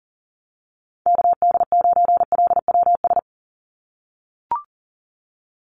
Morse code, heard as a keyed tone, about 39 wpm, 700 Hz. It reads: KD9LWS
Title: USA Call Signs 40wpm